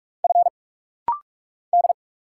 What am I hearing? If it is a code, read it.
Morse code, heard as a keyed tone, about 44 wpm, 700 Hz. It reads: F D
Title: Single Letters - Rapid-Fire 45wpm